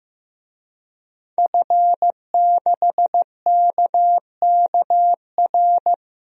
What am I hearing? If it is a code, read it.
Morse code, heard as a keyed tone, about 15 wpm, 700 Hz. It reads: F6KKR
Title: DXCC-100 Call Signs - 5 Characters - 15wpm